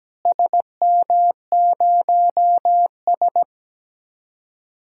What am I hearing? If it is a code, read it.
Morse code, heard as a keyed tone, about 17 wpm, 700 Hz. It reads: SM0S
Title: DXCC-100 Call Signs - 4 Characters - Repeated 2x - 17wpm